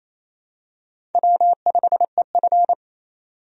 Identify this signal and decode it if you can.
Morse code, heard as a keyed tone, about 28 wpm, 700 Hz. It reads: W5EF
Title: USA Call Signs - 4 Characters - 28wpm